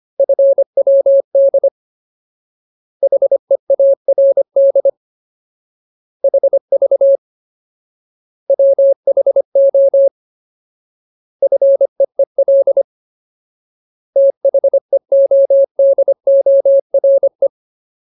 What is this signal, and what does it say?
Morse code, heard as a keyed tone, about 25 wpm, 550 Hz. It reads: FWD HEARD HV WHO FEEL THEODORE
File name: Ultimate Word List / 25WPM Ultimate